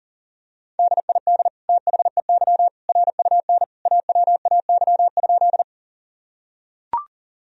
Morse code, heard as a keyed tone, about 40 wpm, 700 Hz. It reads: DID THEY RUN AWAY?
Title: Sentences from Top 1000 Words - Review Part 1 - 40wpm